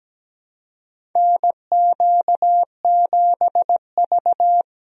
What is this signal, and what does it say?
Morse code, heard as a keyed tone, about 17 wpm, 700 Hz. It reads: NQ7V
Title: USA Call Signs - 4 Characters - 17wpm